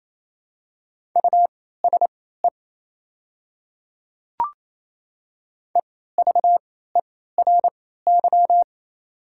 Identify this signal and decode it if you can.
Morse code, heard as a keyed tone, about 27 wpm, spaced as for 16 wpm, 690 Hz. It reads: USE EVERY
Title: 3x Character Spacing - Top 300 Words - 28wpm